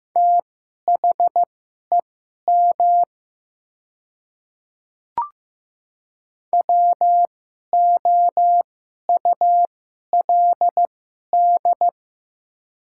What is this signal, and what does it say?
Morse code, heard as a keyed tone, about 15 wpm, spaced as for 11 wpm, 700 Hz. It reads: THEM WOULD
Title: 2x Character Spacing - Top 200 Words - 15wpm